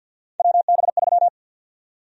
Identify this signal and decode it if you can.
Morse code, heard as a keyed tone, about 52 wpm, 700 Hz. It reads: WB3
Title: USA Call Sign Prefixes - 50wpm